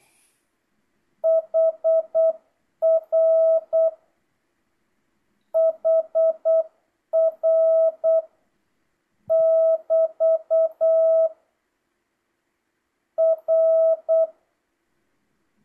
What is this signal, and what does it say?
Morse code, heard as a keyed tone, about 8 wpm, 1300 Hz. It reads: HR HR <BT> R